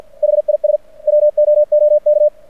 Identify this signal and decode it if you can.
Morse code, heard as a keyed tone, about 49 wpm, 600 Hz. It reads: HIS WOOK
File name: archived / CRSnet